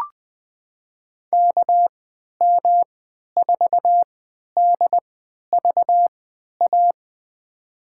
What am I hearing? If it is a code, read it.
Morse code, heard as a keyed tone, about 20 wpm, spaced as for 11 wpm, 700 Hz. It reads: KM4DVA
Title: USA Call Signs - 6 Characters - 3x Character Spacing - 20wpm